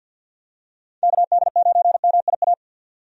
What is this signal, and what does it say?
Morse code, heard as a keyed tone, about 48 wpm, 700 Hz. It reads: KD9MIA
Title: USA Call Signs - 6 Characters - 50wpm